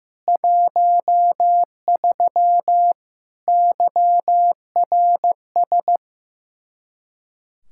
Morse code, heard as a keyed tone, about 16 wpm, 700 Hz. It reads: 13 YRS
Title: Practice QSO - Exchange 3 - More Info - 15wpm